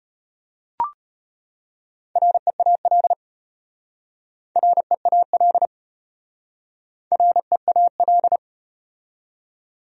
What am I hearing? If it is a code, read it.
Morse code, heard as a keyed tone, about 33 wpm, 700 Hz. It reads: REAL REAL REAL